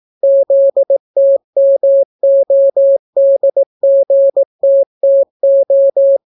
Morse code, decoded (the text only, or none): ZTMODGTTO